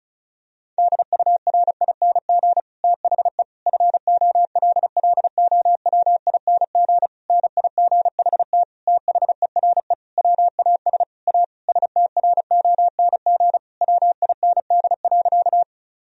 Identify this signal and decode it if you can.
Morse code, heard as a keyed tone, about 36 wpm, 700 Hz. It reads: DURING THE FOLLOWING NIGHT THERE WAS A STRONG WIND.
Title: Sentences from Top 1000 Words - Review Part 4 - 35wpm